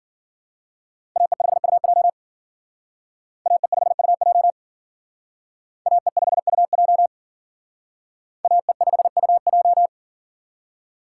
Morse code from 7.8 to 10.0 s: AE5UJ

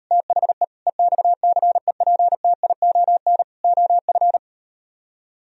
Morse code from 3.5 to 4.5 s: OF

